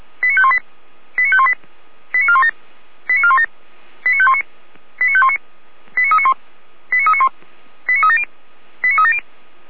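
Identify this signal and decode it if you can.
Five-tone selective call (ZVEI), heard as quick runs of five tones, tones of about 80 ms. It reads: 89518 89518 89417 89417 89519 89519 89391 89391 89370 89370
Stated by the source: SDR recording of a ZVEI five-tone selective call, address 89462.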